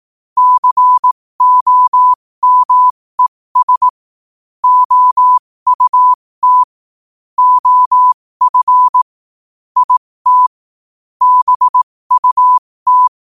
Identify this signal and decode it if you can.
Morse code, heard as a keyed tone, about 18 wpm, spaced as for 15 wpm, 1000 Hz. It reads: COMES OUT OF IT BUT
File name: qotdfiles / 15wpm